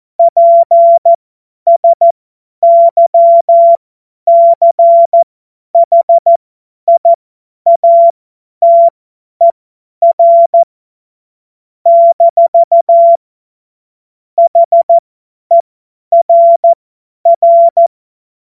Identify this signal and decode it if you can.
Morse code, heard as a keyed tone, about 14 wpm, spaced as for 10 wpm, 680 Hz. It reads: PSYCHIATER - HERR
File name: Morsekurs / Witze / Witz 24 Juni 2025 50 F70